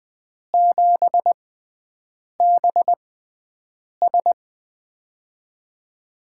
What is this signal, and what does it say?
Morse code, heard as a keyed tone, about 20 wpm, spaced as for 7 wpm, 700 Hz. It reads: 7BS
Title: USA Call Signs - 5 Characters - 6x Character Spacing - 20wpm